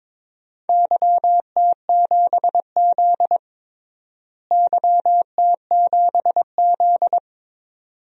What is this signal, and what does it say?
Morse code, heard as a keyed tone, about 22 wpm, 700 Hz. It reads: YT7Z YT7Z
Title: DXCC-100 Call Signs - 4 Characters - Repeated 2x - 22wpm